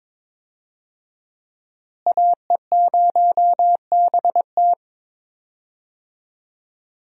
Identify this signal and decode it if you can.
Morse code, heard as a keyed tone, about 22 wpm, 700 Hz. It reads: AE0BT
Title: USA Call Signs - 5 Characters - Repeated 2x - 22wpm